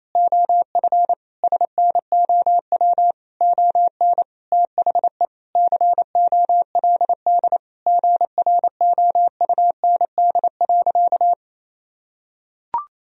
Morse code, heard as a keyed tone, about 28 wpm, 700 Hz. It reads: OF SNOW ON THE COLD GROUND.
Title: Sentences from Top 1000 Words - Review Part 3 - 28wpm